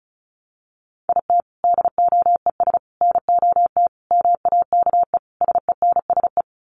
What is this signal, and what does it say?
Morse code, heard as a keyed tone, about 34 wpm, 700 Hz. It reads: IT DOES NOT MAKE SENSE
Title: Sets of 5 Words Encore2 35wpm